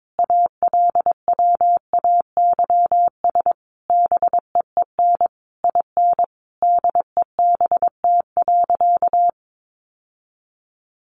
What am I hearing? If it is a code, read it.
Morse code, heard as a keyed tone, about 22 wpm, 700 Hz. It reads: ALWAYS BEEN IN DEBT.
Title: Sentences from Top 1300 Words 22wpm